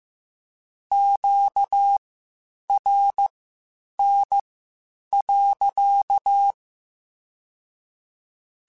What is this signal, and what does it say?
Morse code, heard as a keyed tone, about 15 wpm, spaced as for 8 wpm, 800 Hz. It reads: QRN.
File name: Q Calls / Q Calls-mp3s / Q Calls sppc @13